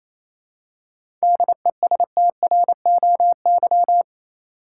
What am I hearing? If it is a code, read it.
Morse code, heard as a keyed tone, about 27 wpm, 700 Hz. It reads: DESTROY
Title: Seven Letter Words - 28wpm